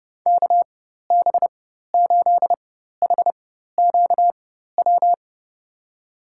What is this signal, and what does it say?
Morse code, heard as a keyed tone, about 30 wpm, spaced as for 14 wpm, 700 Hz. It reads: KB8HQW
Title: USA Call Signs - 6 Characters - 4x Character Spacing - 30wpm